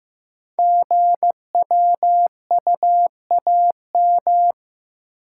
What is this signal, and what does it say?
Morse code, heard as a keyed tone, about 15 wpm, 700 Hz. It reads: GWUAM
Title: Code Groups - 5 Letters - 15wpm